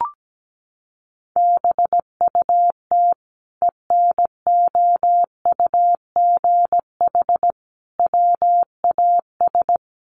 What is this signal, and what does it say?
Morse code, heard as a keyed tone, about 17 wpm, 700 Hz. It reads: BUT ENOUGH WAS
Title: Sentences from Top 1100 Words 17wpm